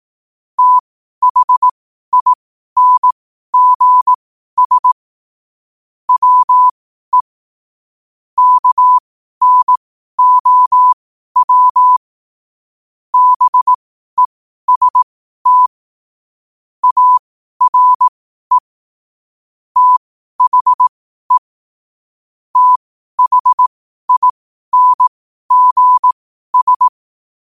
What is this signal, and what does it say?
Morse code, heard as a keyed tone, about 18 wpm, spaced as for 12 wpm, 1000 Hz. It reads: THINGS WE KNOW BEST ARE THE THINGS
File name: qotdfiles / 12wpm